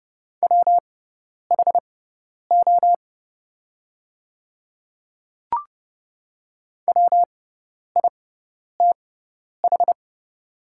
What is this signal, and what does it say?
Morse code, heard as a keyed tone, about 30 wpm, spaced as for 10 wpm, 700 Hz. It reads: WHO WITH
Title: 6x Character Spacing - Top 100 Words - 30wpm